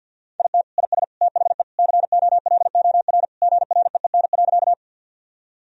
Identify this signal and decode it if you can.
Morse code, heard as a keyed tone, about 52 wpm, 700 Hz. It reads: IT IS THE COLOR GREEN.
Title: Sentences from Top 1000 Words - Review Part 2 - 50wpm